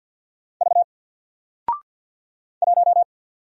Morse code, heard as a keyed tone, about 54 wpm, 700 Hz. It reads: V 1
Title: Single Letter-Number Alt 50wpm